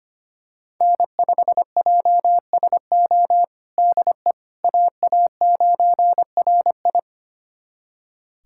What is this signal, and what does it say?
Morse code, heard as a keyed tone, about 25 wpm, 700 Hz. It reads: N5JSO DE AA9RI
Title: USA Callsign DE Callsign - 5 Characters - 25wpm